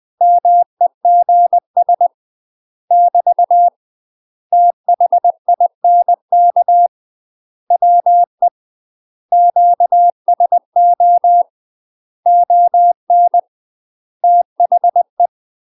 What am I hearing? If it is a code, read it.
Morse code, heard as a keyed tone, about 20 wpm, 700 Hz. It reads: MEGS <BT> THINK WE QSO ON THE
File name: UK MASTER QSOs TRAINING/20WPM